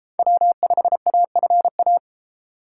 Morse code, heard as a keyed tone, about 32 wpm, 700 Hz. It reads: W5AFA